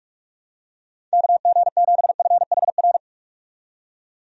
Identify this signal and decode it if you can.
Morse code, heard as a keyed tone, about 47 wpm, 700 Hz. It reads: KG7FHR
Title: USA Call Signs - 6 Characters - 45wpm